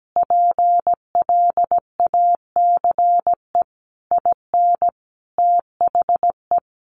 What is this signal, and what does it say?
Morse code, heard as a keyed tone, about 17 wpm, 700 Hz. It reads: PLACE IN THE
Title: Sentences from Top 1300 Words 17wpm